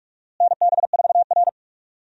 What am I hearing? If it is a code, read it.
Morse code, heard as a keyed tone, about 46 wpm, 700 Hz. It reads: NB4R
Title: USA Call Signs - 4 Characters - Repeated 2x - 45wpm